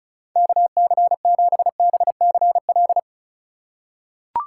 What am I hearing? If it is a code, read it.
Morse code, heard as a keyed tone, about 36 wpm, 700 Hz. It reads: KC7BCL
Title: USA Call Signs - 6 Characters - 4x Character Spacing - 35wpm